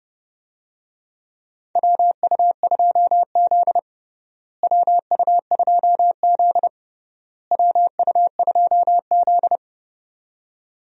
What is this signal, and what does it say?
Morse code, heard as a keyed tone, about 31 wpm, 700 Hz. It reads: WU2Z WU2Z WU2Z
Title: USA Call Signs - 4 Characters - Repeated 3x - 30wpm